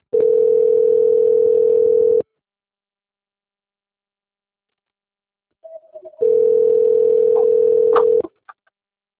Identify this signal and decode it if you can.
Ringback tone (North America), 440 + 480 Hz.
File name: out- - - - - .981